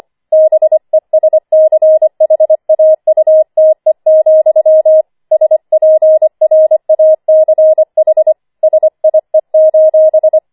Morse code, heard as a keyed tone, about 25 wpm, 630 Hz. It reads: BESCHAUTE, SPRACH SIE:
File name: 25wpm